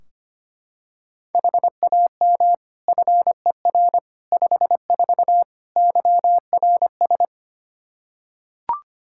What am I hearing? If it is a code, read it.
Morse code, heard as a keyed tone, about 25 wpm, 700 Hz. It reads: HAM FER 54 YRS
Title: K1USN SST QSO Practice 25wpm